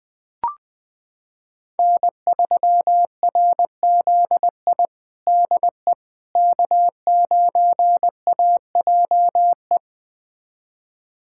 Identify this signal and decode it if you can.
Morse code, heard as a keyed tone, about 20 wpm, 700 Hz. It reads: N3RZI DE K9AJE
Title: USA Callsign DE Callsign - 5 Characters - 20wpm